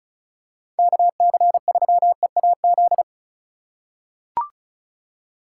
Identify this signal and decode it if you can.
Morse code, heard as a keyed tone, about 36 wpm, 700 Hz. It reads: KC3EAZ
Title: USA Call Signs - 6 Characters - 3x Character Spacing - 35wpm